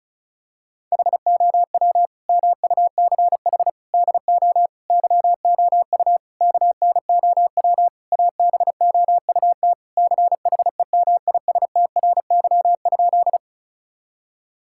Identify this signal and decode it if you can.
Morse code, heard as a keyed tone, about 34 wpm, 700 Hz. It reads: HOW MUCH DO YOU KNOW ABOUT CHEMISTRY?
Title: Sentences from Top 1000 Words - Review Part 6 - 35wpm